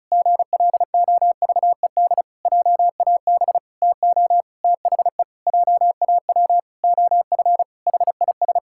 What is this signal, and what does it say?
Morse code, heard as a keyed tone, about 36 wpm, 700 Hz. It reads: GLOVED JAB TO THE JAW OF HIS
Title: Pangram Challenge v2 - 35wpm